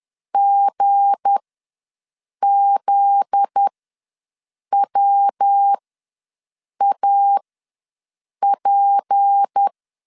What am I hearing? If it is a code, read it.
Morse code, heard as a keyed tone, about 10 wpm, spaced as for 6 wpm, 800 Hz. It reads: GZWAP
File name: LEZIONI / 7 GF9Z / 7 GF9Z ws 10wpm 3919